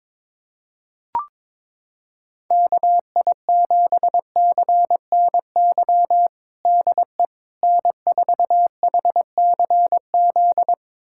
Morse code, heard as a keyed tone, about 22 wpm, 700 Hz. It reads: KI7CNY DE N4HCZ
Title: USA Callsign DE Callsign 22wpm